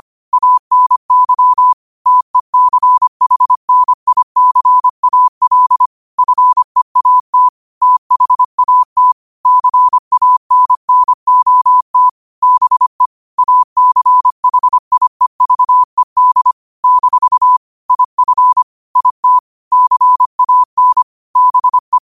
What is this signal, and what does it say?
Morse code, heard as a keyed tone, about 25 wpm, 1000 Hz. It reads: ANY TECHNICAL FEAT THAT CANNOT BE ACHIEVED - IF IT CAN BE
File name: qotdfiles / 25wpm